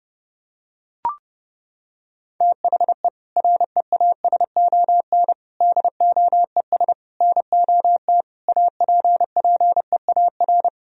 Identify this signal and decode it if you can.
Morse code, heard as a keyed tone, about 30 wpm, 700 Hz. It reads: THE REASON DOES NOT APPEAR
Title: Sentences from Top 1000 Words - Review Part 3 - 30wpm